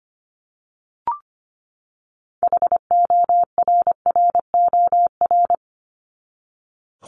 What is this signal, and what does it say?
Morse code, heard as a keyed tone, about 25 wpm, 700 Hz. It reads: HORROR
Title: Top 1501-1750 Words 25wpm